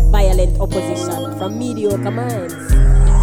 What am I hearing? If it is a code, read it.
Speech. There is no code.